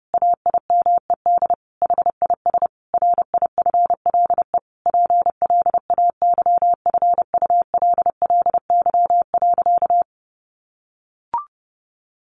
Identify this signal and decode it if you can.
Morse code, heard as a keyed tone, about 30 wpm, 700 Hz. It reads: AIMED HIS RIFLE PLAYFULLY.
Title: A Sound Of Thunder 30wpm